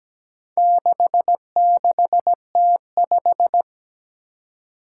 Morse code, heard as a keyed tone, about 17 wpm, 700 Hz. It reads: 66T5
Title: Four Challenging Letter-Number Combinations 17wpm